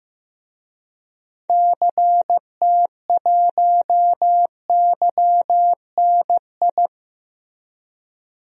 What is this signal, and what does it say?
Morse code, heard as a keyed tone, about 16 wpm, 700 Hz. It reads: CT1YNI
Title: DXCC-100 Call Signs - 6 Characters - Repeated 2x - 15wpm